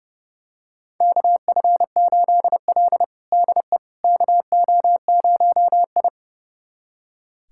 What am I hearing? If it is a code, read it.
Morse code, heard as a keyed tone, about 30 wpm, 700 Hz. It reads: KF8L DE KO0I
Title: USA Callsign DE Callsign - 4 Characters - 30wpm